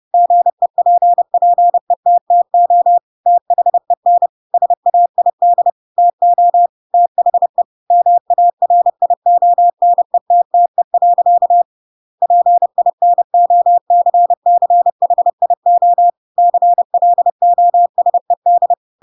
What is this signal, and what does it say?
Morse code, heard as a keyed tone, about 30 wpm, 700 Hz. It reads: GEPPETTO THEN SAID TO THE MARIONETTE. PINOCCHIO CLOSED
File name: Pinocchio / Pinocchio ch8 30 30 ik1hge